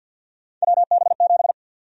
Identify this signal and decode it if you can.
Morse code, heard as a keyed tone, about 51 wpm, 700 Hz. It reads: WB7
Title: USA Call Sign Prefixes - Repeated 2x - 50wpm